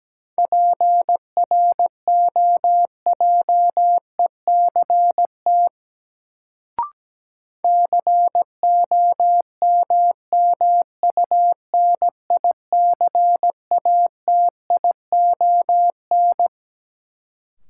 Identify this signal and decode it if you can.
Morse code, heard as a keyed tone, about 17 wpm, 700 Hz. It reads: PROJECT COMMUNICATION
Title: Sentences from Top 1400 Words - New Word Drill Alt - 17wpm